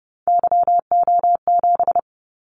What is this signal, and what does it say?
Morse code, heard as a keyed tone, about 30 wpm, 700 Hz. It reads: YO7
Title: Callsign Prefixes for Top 25 DXCC - 30wpm